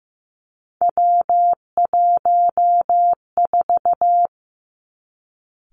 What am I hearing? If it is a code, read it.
Morse code, heard as a keyed tone, about 15 wpm, 700 Hz. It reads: W14
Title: Three Challenging Letter-Number Combinations 15wpm